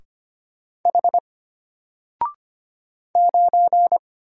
Morse code, heard as a keyed tone, about 25 wpm, 700 Hz. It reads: H 9
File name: Single-Letter-Number-25wpm